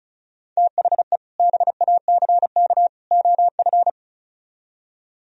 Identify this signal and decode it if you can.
Morse code, heard as a keyed tone, about 42 wpm, 700 Hz. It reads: THE BACK OF
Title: Speed-Racing - Sets of 3 Words 35wpm